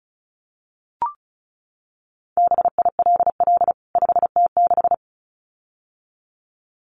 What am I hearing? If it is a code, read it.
Morse code, heard as a keyed tone, about 34 wpm, 700 Hz. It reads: BILL 5T6
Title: CWT Exchange 35wpm